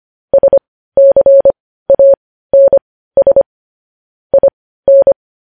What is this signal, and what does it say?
Morse code, heard as a keyed tone, about 25 wpm, spaced as for 15 wpm, 550 Hz. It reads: SCANS IN
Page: news-headlines-in-morse-code-at-15-wpm